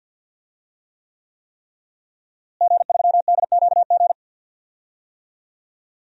Morse code, heard as a keyed tone, about 52 wpm, 700 Hz. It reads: G3DQG
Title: DXCC-100 Call Signs - 5 Characters - Repeated 3x - 50wpm